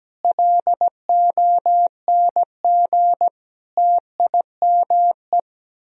Morse code, heard as a keyed tone, about 17 wpm, 700 Hz. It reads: LONG TIME
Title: Speed-Racing - Sets of 5 Words 17wpm